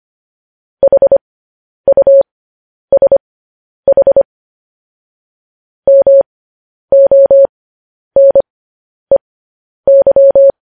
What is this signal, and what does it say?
Morse code, heard as a keyed tone, about 25 wpm, spaced as for 10 wpm, 550 Hz. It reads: HUSH MONEY
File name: podcast / News headlines-1x-25x10